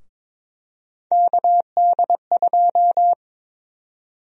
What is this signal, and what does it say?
Morse code, heard as a keyed tone, about 22 wpm, 700 Hz. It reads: KD2
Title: USA Call Sign Prefixes - Repeated 2x - 22wpm